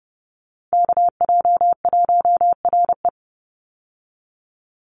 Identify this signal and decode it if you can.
Morse code, heard as a keyed tone, about 30 wpm, 700 Hz. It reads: KJ1RE
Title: Call Signs - 30wpm